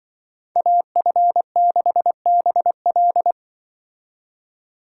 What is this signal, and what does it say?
Morse code, heard as a keyed tone, about 25 wpm, 700 Hz. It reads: AF6BL
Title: Speed Racing - USA 2x2 Call Signs 20wpm